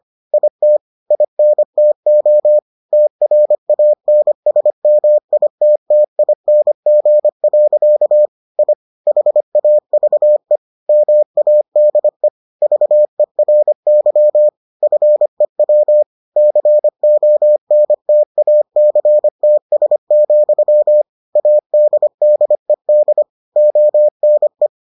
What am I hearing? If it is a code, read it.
Morse code, heard as a keyed tone, about 25 wpm, 600 Hz. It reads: IT INTO TRANSMITTING. I HAVE MADE VERY FEW CONTACTS, ADDED ONE